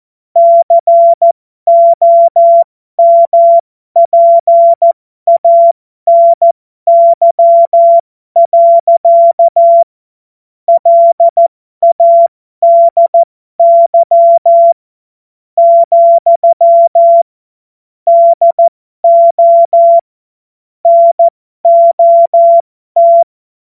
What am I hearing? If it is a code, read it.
Morse code, heard as a keyed tone, about 14 wpm, spaced as for 12 wpm, 680 Hz. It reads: COMPANY. LADY, DO NOT
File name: Morsekurs / Witze